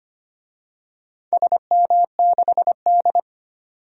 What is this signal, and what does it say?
Morse code, heard as a keyed tone, about 25 wpm, 700 Hz. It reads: SM6D
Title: DXCC-100 Call Signs - 4 Characters - 25wpm